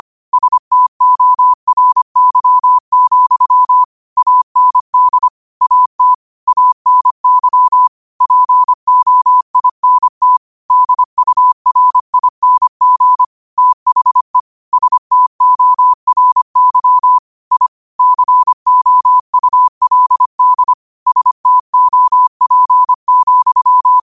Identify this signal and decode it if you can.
Morse code, heard as a keyed tone, about 25 wpm, 1000 Hz. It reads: STORY, AND AT ANY POINT DURING THE STORY I COULD STOP,